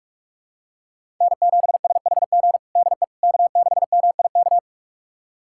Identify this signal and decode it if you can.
Morse code, heard as a keyed tone, about 46 wpm, 700 Hz. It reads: N7SHG DE K6MIK
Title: USA Callsign DE Callsign - 5 Characters - 45wpm